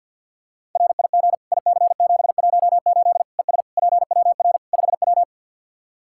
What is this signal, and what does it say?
Morse code, heard as a keyed tone, about 53 wpm, 700 Hz. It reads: RIG IC718 ES PWR 5W
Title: Practice QSO - Exchange 2 - Info - 50wpm